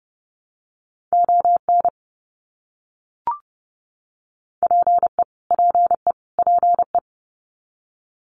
Morse code, heard as a keyed tone, about 30 wpm, 700 Hz. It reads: ON PE PE PE
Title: Canadian Provinces and Territories - Repeated 3x - 30wpm